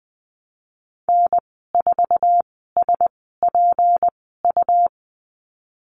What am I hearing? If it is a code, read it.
Morse code, heard as a keyed tone, about 20 wpm, spaced as for 15 wpm, 700 Hz. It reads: N4SPU
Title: 219 Lesson 35g - Number 0 - Call Signs 20wpm